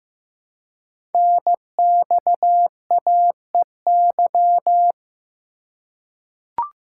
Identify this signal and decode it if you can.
Morse code, heard as a keyed tone, about 15 wpm, 700 Hz. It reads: NXAEY
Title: Code Groups - 5 Letters - 15wpm